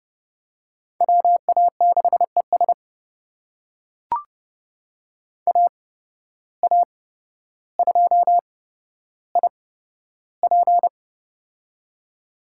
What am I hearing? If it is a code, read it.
Morse code, heard as a keyed tone, about 30 wpm, spaced as for 13 wpm, 700 Hz. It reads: WA6ES AA2IP
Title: USA Call Signs - 5 Characters - 8x Character Spacing - 30wpm